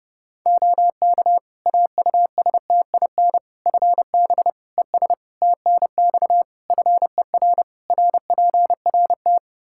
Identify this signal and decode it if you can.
Morse code, heard as a keyed tone, about 31 wpm, 700 Hz. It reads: OK AUSTIN FB ES TNX FER RPRT
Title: Practice QSO - Exchange 2 - Info - 30wpm